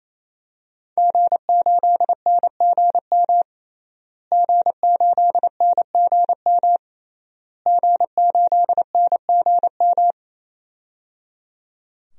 Morse code, heard as a keyed tone, about 28 wpm, 700 Hz. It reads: G8NGM G8NGM G8NGM
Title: DXCC-100 Call Signs - 5 Characters - Repeated 3x - 28wpm